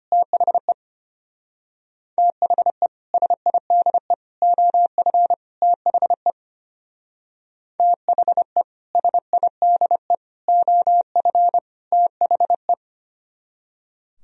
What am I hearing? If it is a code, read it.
Morse code, heard as a keyed tone, about 32 wpm, 700 Hz. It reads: THE THE SIDE OF THE THE SIDE OF THE